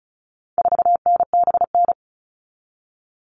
Morse code, heard as a keyed tone, about 34 wpm, 700 Hz. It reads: 4NBN
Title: All Cut Numbers - 4-Digit 35wpm